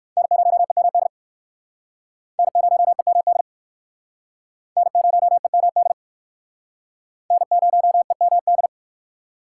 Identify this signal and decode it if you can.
Morse code, heard as a keyed tone, about 60 wpm, 700 Hz. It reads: N0EMD N0EMD N0EMD N0EMD